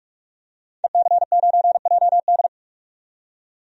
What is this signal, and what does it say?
Morse code, heard as a keyed tone, about 47 wpm, 700 Hz. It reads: EC9JD
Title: DXCC-100 Call Signs - 5 Characters - 45wpm